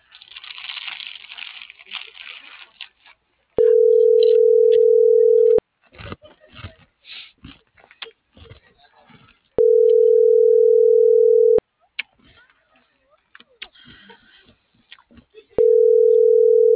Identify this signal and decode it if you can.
Ringback tone (North America), 440 + 480 Hz.